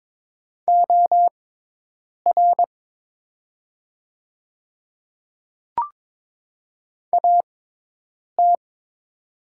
Morse code, heard as a keyed tone, about 22 wpm, spaced as for 8 wpm, 700 Hz. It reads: OR AT
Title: 6x Character Spacing - Top 100 Words - 22wpm